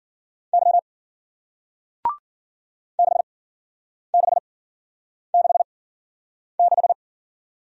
Morse code, heard as a keyed tone, about 52 wpm, spaced as for 9 wpm, 700 Hz. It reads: Y 6666